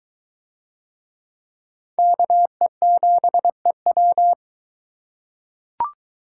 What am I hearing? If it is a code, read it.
Morse code, heard as a keyed tone, about 22 wpm, 700 Hz. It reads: KE7EW